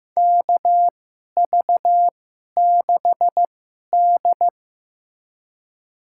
Morse code, heard as a keyed tone, about 16 wpm, spaced as for 11 wpm, 700 Hz. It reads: KV6D
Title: USA Call Signs - 4 Characters - 2x Character Spacing - 15wpm